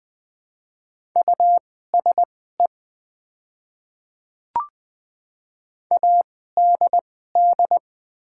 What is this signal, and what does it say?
Morse code, heard as a keyed tone, about 20 wpm, spaced as for 15 wpm, 700 Hz. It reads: USE ADD
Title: 2x Character Spacing - Top 200 Words - 20wpm